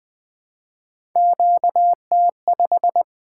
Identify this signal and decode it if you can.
Morse code, heard as a keyed tone, about 20 wpm, 700 Hz. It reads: QT5
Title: Three Letter-Number Combinations - 20wpm